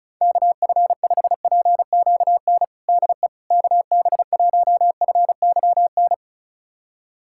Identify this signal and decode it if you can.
Morse code, heard as a keyed tone, about 34 wpm, 700 Hz. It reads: KF5PQN DE KB1FYN
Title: USA Callsign DE Callsign - 6 Characters - 35wpm